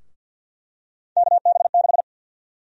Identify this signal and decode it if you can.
Morse code, heard as a keyed tone, about 52 wpm, 700 Hz. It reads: KB6
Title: USA Call Sign Prefixes - Repeated 2x - 50wpm